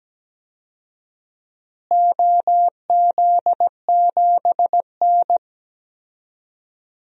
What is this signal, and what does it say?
Morse code, heard as a keyed tone, about 17 wpm, 700 Hz. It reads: OZ7N
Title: DXCC-100 Call Signs - 4 Characters - Repeated 3x - 17wpm